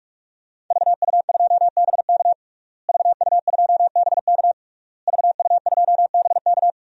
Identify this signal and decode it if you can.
Morse code, heard as a keyed tone, about 46 wpm, 700 Hz. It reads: VU2BK VU2BK VU2BK
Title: DXCC-100 Call Signs - 5 Characters - Repeated 3x - 45wpm